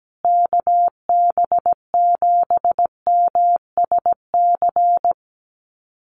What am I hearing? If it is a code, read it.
Morse code, heard as a keyed tone, about 17 wpm, 700 Hz. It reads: KB7MSC
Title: USA Call Signs Repeated 2x 17wpm